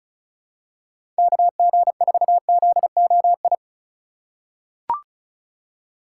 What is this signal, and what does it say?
Morse code, heard as a keyed tone, about 36 wpm, 700 Hz. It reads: KG4ZOI
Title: USA Call Signs - 6 Characters - 2x Character Spacing - 35wpm